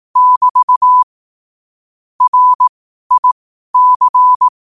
Morse code, heard as a keyed tone, about 18 wpm, spaced as for 12 wpm, 1000 Hz. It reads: <BT> RIC
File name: qotdfiles / 12wpm